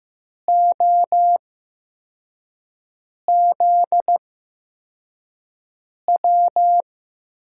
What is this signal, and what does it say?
Morse code, heard as a keyed tone, about 15 wpm, spaced as for 4 wpm, 700 Hz. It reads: OZW
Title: USA Call Signs - 8x Character Spacing - 15wpm